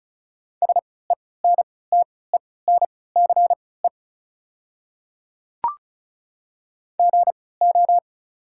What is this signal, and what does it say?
Morse code, heard as a keyed tone, about 36 wpm, spaced as for 20 wpm, 700 Hz. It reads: SENTENCE GO
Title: 3x Character Spacing - Top 200 Words - 35wpm